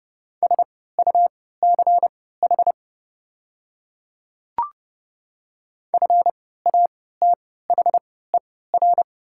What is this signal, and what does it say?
Morse code, heard as a keyed tone, about 30 wpm, spaced as for 17 wpm, 700 Hz. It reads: SUCH FATHER
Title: 3x Character Spacing - Top 200 Words - 30wpm